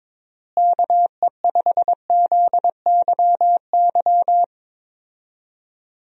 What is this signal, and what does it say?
Morse code, heard as a keyed tone, about 21 wpm, 700 Hz. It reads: KE5ZYY
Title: USA Call Signs - 6 Characters - Repeated 2x - 22wpm